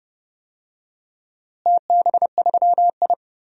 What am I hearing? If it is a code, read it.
Morse code, heard as a keyed tone, about 30 wpm, 700 Hz. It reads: TB3I
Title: DXCC-100 Call Signs - 4 Characters - Repeated 3x - 30wpm